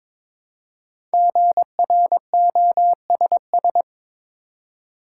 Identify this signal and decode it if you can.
Morse code, heard as a keyed tone, about 22 wpm, 700 Hz. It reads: GROSS
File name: 5 Letter Words - First 3 Letters Given - 22wpm